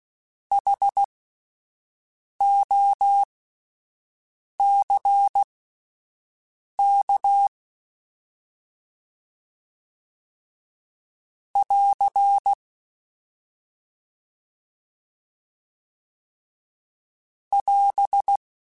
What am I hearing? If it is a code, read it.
Morse code, heard as a keyed tone, about 16 wpm, spaced as for 5 wpm, 800 Hz. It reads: HOCK <AR> <AS>
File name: Alphabetical / Alphabetical mp3s / Hs @5